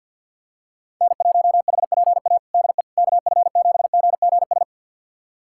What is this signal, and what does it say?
Morse code, heard as a keyed tone, about 46 wpm, 700 Hz. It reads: N1HPA DE KF7GGS
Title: USA Callsign DE Callsign 50wpm